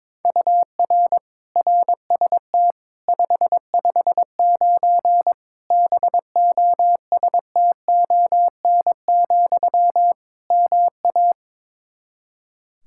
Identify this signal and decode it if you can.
Morse code, heard as a keyed tone, about 22 wpm, 700 Hz. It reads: UR RST 559 BOSTON, MA
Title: Practice QSO - Exchange 1 - Beginning - 22wpm